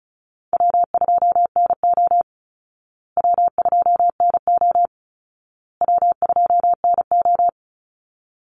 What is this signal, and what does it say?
Morse code, heard as a keyed tone, about 34 wpm, 700 Hz. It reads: W2NO W2NO W2NO
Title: Easy Call Signs Repeated 3x - 35wpm